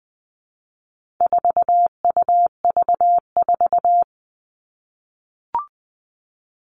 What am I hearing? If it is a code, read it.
Morse code, heard as a keyed tone, about 20 wpm, 700 Hz. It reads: 4UV4
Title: All Cut Numbers - 4-Digit 20wpm